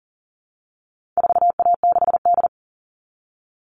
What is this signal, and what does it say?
Morse code, heard as a keyed tone, about 40 wpm, 700 Hz. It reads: 4A6D
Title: All Cut Numbers - 4-Digit 40wpm